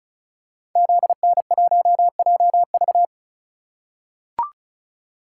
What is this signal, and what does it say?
Morse code, heard as a keyed tone, about 36 wpm, 700 Hz. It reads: ZN1JV
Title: Code Groups - 5 Letter:Numbers - 35wpm